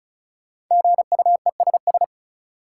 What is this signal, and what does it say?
Morse code, heard as a keyed tone, about 36 wpm, 700 Hz. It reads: GUESS
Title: Five Letter Words - 35wpm